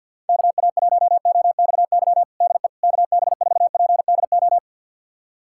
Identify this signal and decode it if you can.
Morse code, heard as a keyed tone, about 52 wpm, 700 Hz. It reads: KA1OXY DE KB4PDO
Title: USA Callsign DE Callsign - 6 Characters - 50wpm